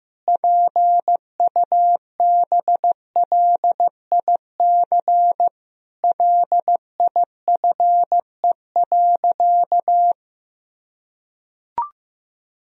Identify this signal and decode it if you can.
Morse code, heard as a keyed tone, about 15 wpm, 700 Hz. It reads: PUBLIC LIFE.
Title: Sentences from Top 1000 Words - Review Part 3 - 15wpm